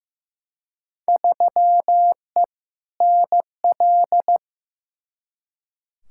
Morse code, heard as a keyed tone, about 15 wpm, 700 Hz. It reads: 3E NL
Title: Field Day Exchanges Encore - 15wpm